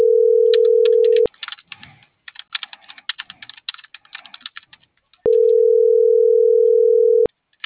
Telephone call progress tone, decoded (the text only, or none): ringback tone